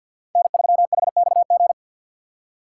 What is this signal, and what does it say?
Morse code, heard as a keyed tone, about 51 wpm, 700 Hz. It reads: N3HXG
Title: USA Call Signs - 6x Character Spacing - 50wpm